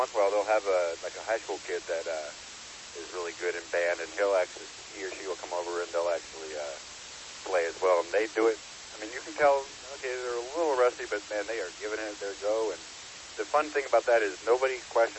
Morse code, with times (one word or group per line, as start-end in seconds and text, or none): none